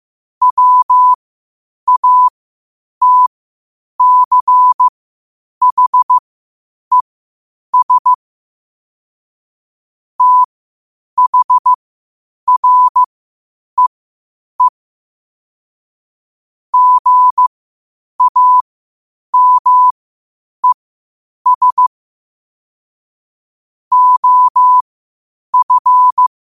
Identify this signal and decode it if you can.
Morse code, heard as a keyed tone, about 15 wpm, spaced as for 8 wpm, 1000 Hz. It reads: WATCHES THREE GAMES OF
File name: qotdfiles / 8wpm